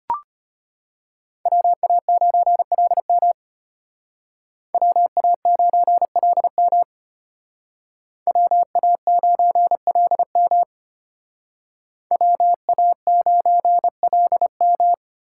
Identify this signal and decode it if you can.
Morse code, heard as a keyed tone, about 32 wpm, 700 Hz. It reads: WA9LM WA9LM WA9LM WA9LM